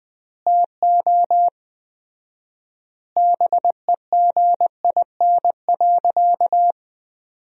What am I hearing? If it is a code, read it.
Morse code, heard as a keyed tone, about 20 wpm, 700 Hz. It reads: TO BEGIN.
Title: 4x Word-Spacing - Sentences from Top 300 Words - 20wpm